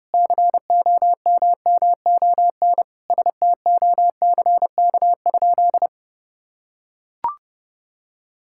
Morse code, heard as a keyed tone, about 32 wpm, 700 Hz. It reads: COMMON STOCK?
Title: Sentences from Top 1000 Words - Review Part 5 - 30wpm